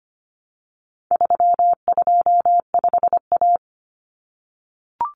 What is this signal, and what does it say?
Morse code, heard as a keyed tone, about 25 wpm, 700 Hz. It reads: 325A
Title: Common Cut Numbers - 4-Digit 25wpm